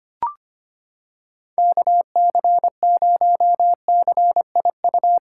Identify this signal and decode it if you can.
Morse code, heard as a keyed tone, about 25 wpm, 700 Hz. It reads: KC0CIU
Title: USA Callsign DE Callsign - 6 Characters - 25wpm